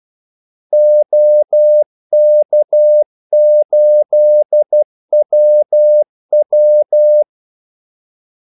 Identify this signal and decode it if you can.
Morse code, heard as a keyed tone, about 12 wpm, 600 Hz. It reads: OK8WW
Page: 2 ekstra ordmellemrum mellem kaldesignalerne